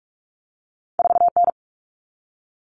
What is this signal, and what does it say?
Morse code, heard as a keyed tone, about 46 wpm, 700 Hz. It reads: 4N
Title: All Cut Numbers - 2-Digit 45wpm